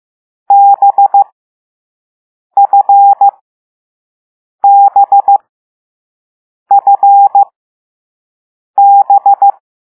Morse code, heard as a keyed tone, about 15 wpm, spaced as for 5 wpm, 800 Hz. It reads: BFBFB
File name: Morse / Level6B